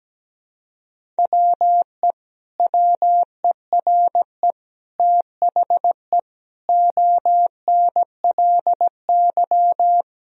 Morse code, heard as a keyed tone, about 17 wpm, 700 Hz. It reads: WE WERE THE ONLY
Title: Sentences from Top 1000 Words - Review Part 1 - 17wpm